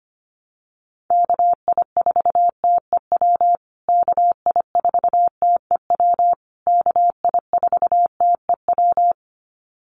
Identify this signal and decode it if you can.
Morse code, heard as a keyed tone, about 25 wpm, 700 Hz. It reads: KI4TEW KI4TEW KI4TEW
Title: USA Call Signs Repeated 3x 25wpm